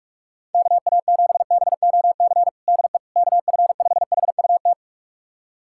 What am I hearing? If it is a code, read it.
Morse code, heard as a keyed tone, about 42 wpm, 700 Hz. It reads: KA7BOC DE KF5HUT
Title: USA Callsign DE Callsign - 6 Characters - 45wpm